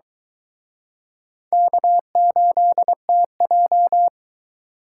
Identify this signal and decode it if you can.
Morse code, heard as a keyed tone, about 23 wpm, 700 Hz. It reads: K8TJ